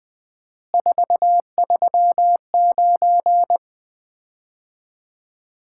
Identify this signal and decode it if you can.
Morse code, heard as a keyed tone, about 20 wpm, 700 Hz. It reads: 439
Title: Common QSO Phrases - 4x Word-Spacing - 20wpm